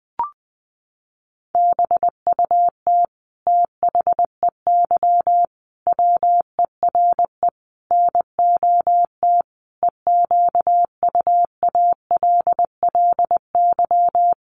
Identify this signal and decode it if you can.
Morse code, heard as a keyed tone, about 20 wpm, 700 Hz. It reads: BUT THEY WERE NOT EQUALLY
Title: Sentences from Top 1400 Words 20wpm